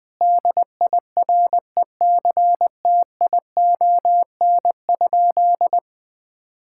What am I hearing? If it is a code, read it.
Morse code, heard as a keyed tone, about 20 wpm, 700 Hz. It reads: DIRECTION?
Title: Sentences from Top 1000 Words - Review Part 5 - 20wpm